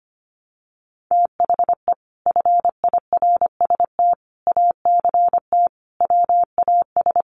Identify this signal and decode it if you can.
Morse code, heard as a keyed tone, about 25 wpm, 700 Hz. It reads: THE FIRST ACT WAS
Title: Sentences from Top 200 Words 25wpm